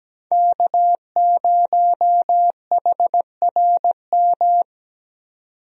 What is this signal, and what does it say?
Morse code, heard as a keyed tone, about 17 wpm, 700 Hz. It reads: K0HRM
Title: USA Call Signs - 5 Characters - Repeated 2x - 17wpm